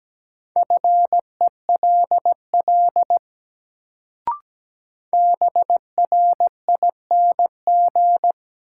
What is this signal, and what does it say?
Morse code, heard as a keyed tone, about 17 wpm, 700 Hz. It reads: FELL BRING
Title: Sets of 5 Words Encore - New Word Drill - 17wpm